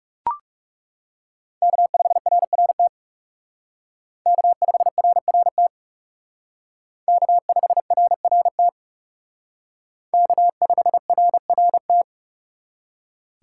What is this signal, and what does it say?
Morse code, heard as a keyed tone, about 37 wpm, 700 Hz. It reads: K5RRT K5RRT K5RRT K5RRT
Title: Speed Racing - USA 1x3 Call Signs 30wpm